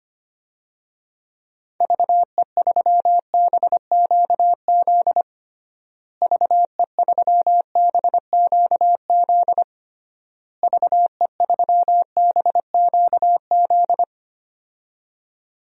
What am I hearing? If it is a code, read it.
Morse code, heard as a keyed tone, about 25 wpm, 700 Hz. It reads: VE3BQZ VE3BQZ VE3BQZ
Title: DXCC-100 Call Signs - 6 Characters - Repeated 3x - 25wpm